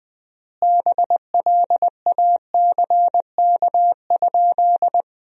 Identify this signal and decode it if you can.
Morse code, heard as a keyed tone, about 20 wpm, 700 Hz. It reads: BLACK?
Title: 3x Word-Spacing - Sentences from Top 400 Words - 20wpm